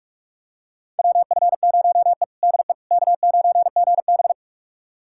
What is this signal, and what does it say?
Morse code, heard as a keyed tone, about 46 wpm, 700 Hz. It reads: WF0E DE K9GB
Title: USA Callsign DE Callsign - 4 Characters - 45wpm